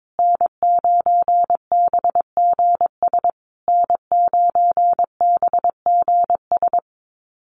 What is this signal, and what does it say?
Morse code, heard as a keyed tone, about 22 wpm, 700 Hz. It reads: N9BGS N9BGS
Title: USA Call Signs Repeated 3x 22wpm